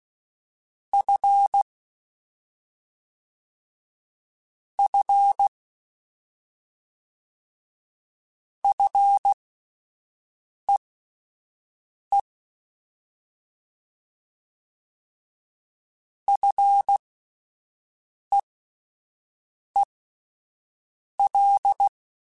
Morse code, heard as a keyed tone, about 16 wpm, spaced as for 5 wpm, 800 Hz. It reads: F F FEE FEEL
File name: Combinations / Combinations-mp3s / Combinations@5